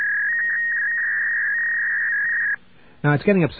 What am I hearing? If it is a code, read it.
Fax machine handshake, with speech.